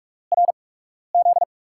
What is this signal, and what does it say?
Morse code, heard as a keyed tone, about 44 wpm, spaced as for 12 wpm, 700 Hz. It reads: RZ
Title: USA Call Signs - 5 Characters - 8x Character Spacing - 45wpm